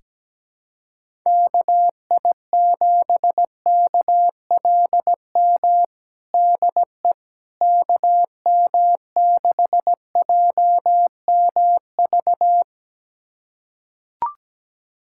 Morse code, heard as a keyed tone, about 17 wpm, 700 Hz. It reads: KI7KLM DE KM6JMV
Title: USA Callsign DE Callsign - 6 Characters - 17wpm